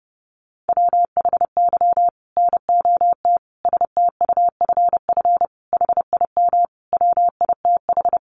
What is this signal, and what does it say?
Morse code, heard as a keyed tone, about 30 wpm, 700 Hz. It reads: WHY NOT STUFF HIM WITH
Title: Sentences from Top 1400 Words 30wpm